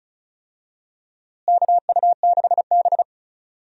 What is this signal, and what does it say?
Morse code, heard as a keyed tone, about 34 wpm, 700 Hz. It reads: KU6B
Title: USA Call Signs - 4 Characters - Repeated 3x - 35wpm